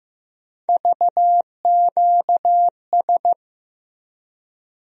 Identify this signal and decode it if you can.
Morse code, heard as a keyed tone, about 15 wpm, 700 Hz. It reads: VQS
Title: USA Call Sign Postfixes - 3 Letters 15wpm